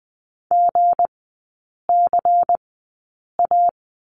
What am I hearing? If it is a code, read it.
Morse code, heard as a keyed tone, about 20 wpm, spaced as for 8 wpm, 700 Hz. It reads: GCA
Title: 198 Lesson 32e - Letter k - Three Characters 20wpm